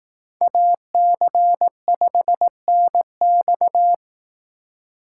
Morse code, heard as a keyed tone, about 18 wpm, 700 Hz. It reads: AC5NX
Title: Speed Racing - USA 2x2 Call Signs 15wpm